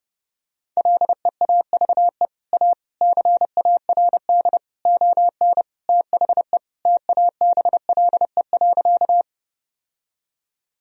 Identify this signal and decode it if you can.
Morse code, heard as a keyed tone, about 30 wpm, 700 Hz. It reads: LEAVE A CARD ON THE TABLE.
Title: Sentences from Top 1000 Words - Review Part 4 - 30wpm